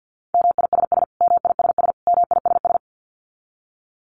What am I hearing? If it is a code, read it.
Morse code, heard as a keyed tone, about 50 wpm, 700 Hz. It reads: MISS MISS MISS
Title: Top 500 Words Repeated 50wpm